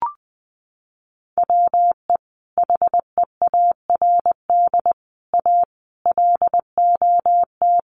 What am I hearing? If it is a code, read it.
Morse code, heard as a keyed tone, about 20 wpm, 700 Hz. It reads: WE HEARD A LOT
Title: Sentences from Top 1100 Words 20wpm